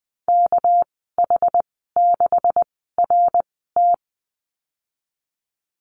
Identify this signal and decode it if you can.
Morse code, heard as a keyed tone, about 20 wpm, spaced as for 14 wpm, 700 Hz. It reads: KH6RT
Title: 245 Lesson 39g - Letter z - Call Signs 20wpm